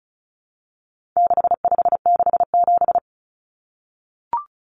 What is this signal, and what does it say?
Morse code, heard as a keyed tone, about 34 wpm, 700 Hz. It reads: 6567
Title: Numbers - 4-Digit 35wpm